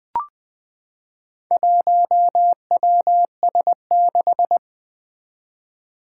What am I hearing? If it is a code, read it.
Morse code, heard as a keyed tone, about 20 wpm, 700 Hz. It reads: 1WS6
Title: Four Challenging Letter-Number Combinations 20wpm